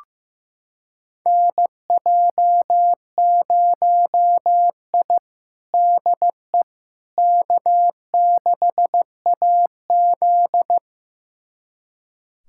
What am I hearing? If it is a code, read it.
Morse code, heard as a keyed tone, about 15 wpm, 700 Hz. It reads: NJ0I DE K6AZ
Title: USA Callsign DE Callsign - 4 Characters - 15wpm